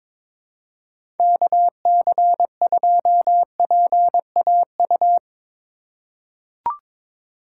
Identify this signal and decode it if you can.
Morse code, heard as a keyed tone, about 22 wpm, 700 Hz. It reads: KC2PAU
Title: USA Call Signs - 6 Characters - 6x Character Spacing - 22wpm